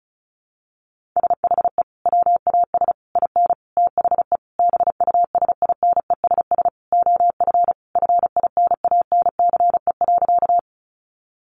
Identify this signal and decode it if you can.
Morse code, heard as a keyed tone, about 34 wpm, 700 Hz. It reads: SHE WAS IN THE BUSINESS OF FINANCE.
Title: Sentences from Top 1500 Words 35wpm